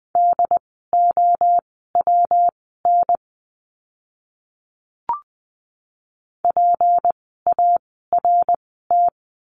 Morse code, heard as a keyed tone, about 20 wpm, spaced as for 15 wpm, 700 Hz. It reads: DOWN PART
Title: 206 Lesson 33f - Letter j - Words 20wpm